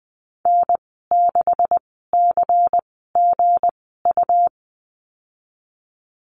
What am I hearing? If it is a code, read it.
Morse code, heard as a keyed tone, about 20 wpm, spaced as for 14 wpm, 700 Hz. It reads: N6CGU
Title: 213 Lesson 34g - Number 8 - Call Signs 20wpm